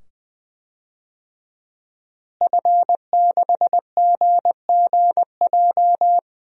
Morse code, heard as a keyed tone, about 20 wpm, 700 Hz. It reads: F6GGJ
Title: DXCC-100 Call Signs - 5 Characters - Repeated 2x - 20wpm